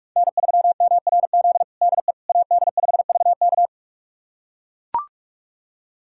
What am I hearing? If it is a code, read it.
Morse code, heard as a keyed tone, about 41 wpm, 700 Hz. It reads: N3MRZ DE AD5VK
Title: USA Callsign DE Callsign - 5 Characters - 45wpm